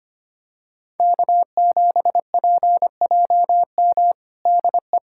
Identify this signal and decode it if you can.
Morse code, heard as a keyed tone, about 25 wpm, 700 Hz. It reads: K7PJM DE
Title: USA Callsign DE Callsign - 5 Characters - 25wpm